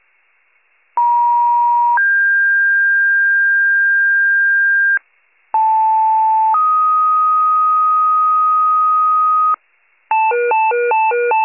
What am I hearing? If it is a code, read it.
Two-tone page, heard as a one-second tone, then a three-second tone, about 950 then 1600 Hz, about 880 then 1250 Hz, with noise. It is heard from 1.0 to 9.5 s.
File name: pager / 76 FIRE / wcalert